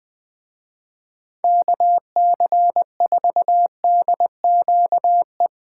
Morse code, heard as a keyed tone, about 20 wpm, 700 Hz. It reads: KC4DQE